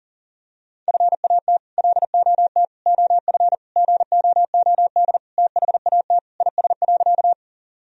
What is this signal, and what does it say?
Morse code, heard as a keyed tone, about 40 wpm, 700 Hz. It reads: FAT LOT OF GOOD THAT IS.
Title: Sentences from Top 1000 Words - Review Part 4 - 40wpm